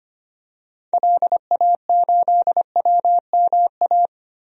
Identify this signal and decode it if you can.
Morse code, heard as a keyed tone, about 25 wpm, 700 Hz. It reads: LA8WMA
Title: DXCC-100 Call Signs - 6 Characters - 25wpm